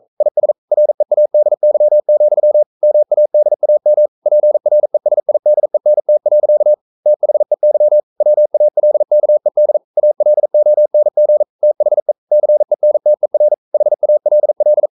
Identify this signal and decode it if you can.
Morse code, heard as a keyed tone, about 42 wpm, 600 Hz. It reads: IS READY, MADAM PRESIDENT. THEY WALKED ALONG THE CENTER HALL
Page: Romanen Never af Ken Follet på engelsk.